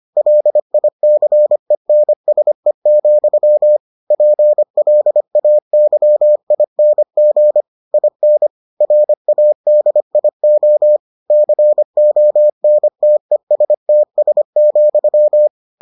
Morse code, heard as a keyed tone, about 25 wpm, 600 Hz. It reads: LICENSE, PLAYING IN RADIO CONTESTS,